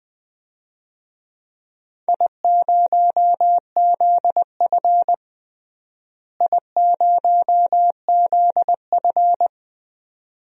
Morse code, heard as a keyed tone, about 20 wpm, 700 Hz. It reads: I0ZF I0ZF
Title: DXCC-100 Call Signs - 4 Characters - Repeated 2x - 20wpm